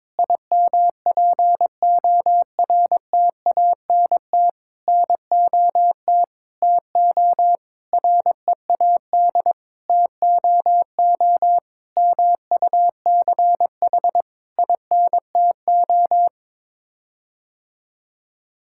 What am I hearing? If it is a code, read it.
Morse code, heard as a keyed tone, about 21 wpm, 700 Hz. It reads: IMPORTANT NOT TO READ TOO MUCH INTO
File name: Sets of 9 Words - 22wpm